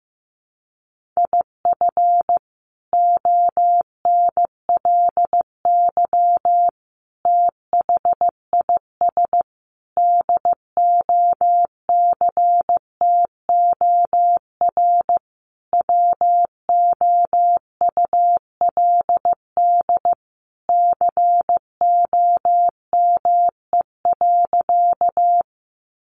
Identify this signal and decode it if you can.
Morse code, heard as a keyed tone, about 15 wpm, 700 Hz. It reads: IF ONLY THIS DOCTOR WOULD COME.
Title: Sentences from Top 1500 Words 15wpm